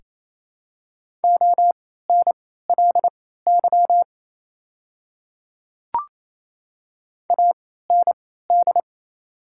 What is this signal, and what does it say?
Morse code, heard as a keyed tone, about 27 wpm, spaced as for 16 wpm, 700 Hz. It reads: ONLY AND
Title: 3x Character Spacing - Top 200 Words - 28wpm